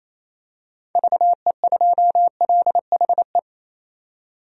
Morse code, heard as a keyed tone, about 28 wpm, 700 Hz. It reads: VE2LHE
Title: DXCC-100 Call Signs - 6 Characters - 28wpm